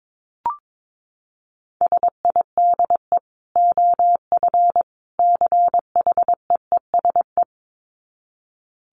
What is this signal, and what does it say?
Morse code, heard as a keyed tone, about 22 wpm, 700 Hz. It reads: SIDE OF CHEESE
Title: Dit Challenge 22wpm